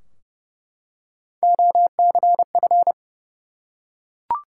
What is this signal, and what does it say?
Morse code, heard as a keyed tone, about 30 wpm, 700 Hz. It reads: OCF
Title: Top 180 QSO Elements 30wpm